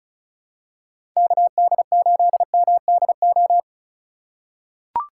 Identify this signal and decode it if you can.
Morse code, heard as a keyed tone, about 36 wpm, 700 Hz. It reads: KD8MDO
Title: USA Call Signs - 3x Character Spacing - 35wpm